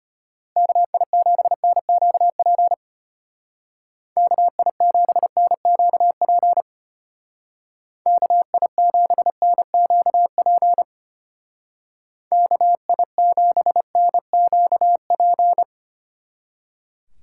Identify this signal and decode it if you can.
Morse code, heard as a keyed tone, about 32 wpm, 700 Hz. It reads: KI7NQP KI7NQP KI7NQP KI7NQP